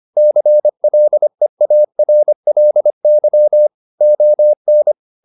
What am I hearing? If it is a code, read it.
Morse code, heard as a keyed tone, about 25 wpm, 600 Hz. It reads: CLEARLY ON